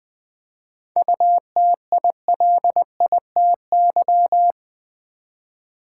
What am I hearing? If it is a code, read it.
Morse code, heard as a keyed tone, about 20 wpm, 690 Hz. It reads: UTILITY
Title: 7 Letter Words - First 3 Letters Given - 20wpm